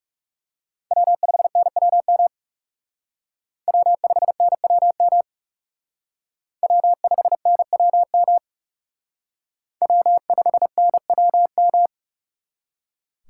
Morse code, heard as a keyed tone, about 38 wpm, 700 Hz. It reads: W5NWM W5NWM W5NWM W5NWM
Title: Speed Racing - USA 1x3 Call Signs 30wpm